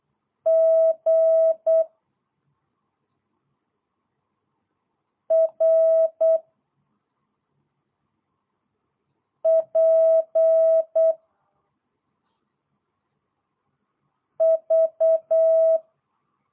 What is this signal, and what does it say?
Morse code, heard as a keyed tone, about 8 wpm, 650 Hz. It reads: G R P V